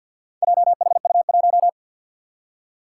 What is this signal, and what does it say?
Morse code, heard as a keyed tone, about 53 wpm, 700 Hz. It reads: JHU1
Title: Four Challenging Letter-Number Combinations 50wpm